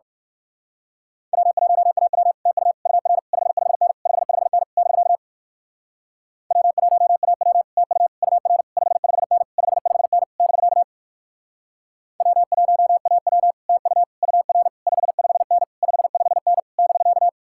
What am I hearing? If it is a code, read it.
Morse code, heard as a keyed tone, about 52 wpm, 700 Hz. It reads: W1AW TU UR 55N 55N <BK> W1AW TU UR 55N 55N <BK> W1AW TU UR 55N 55N <BK>
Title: Speed-Racing Common QSO Phrases - 45wpm